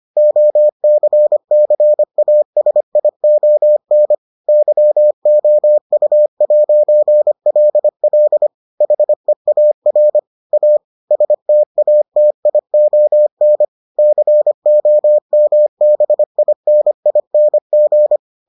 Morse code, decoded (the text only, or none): OCCASION YOU'LL HEAR A STATION COMBINING